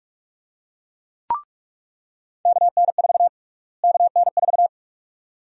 Morse code, heard as a keyed tone, about 41 wpm, 700 Hz. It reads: KN4 KN4
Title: USA Call Sign Prefixes - Repeated 2x - 45wpm